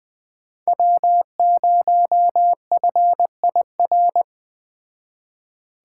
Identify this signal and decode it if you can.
Morse code, heard as a keyed tone, about 20 wpm, 700 Hz. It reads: W0FIR